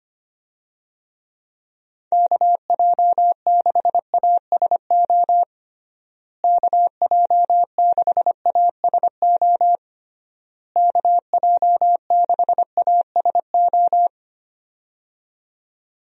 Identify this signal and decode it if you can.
Morse code, heard as a keyed tone, about 25 wpm, 700 Hz. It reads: KJ6ASO KJ6ASO KJ6ASO
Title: USA Call Signs - 6 Characters - Repeated 3x - 25wpm